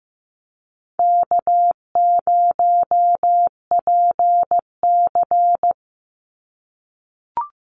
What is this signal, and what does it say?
Morse code, heard as a keyed tone, about 15 wpm, 700 Hz. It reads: K0PC
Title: Easy Call Signs - 15wpm